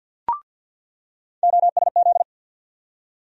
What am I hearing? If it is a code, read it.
Morse code, heard as a keyed tone, about 48 wpm, 700 Hz. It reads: OSZ
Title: USA Call Sign Postfixes - 3 Letters 50wpm